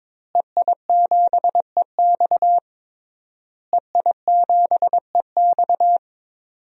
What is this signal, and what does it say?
Morse code, heard as a keyed tone, about 22 wpm, 700 Hz. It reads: EI7EX EI7EX
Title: DXCC-100 Call Signs - 5 Characters - Repeated 3x - 22wpm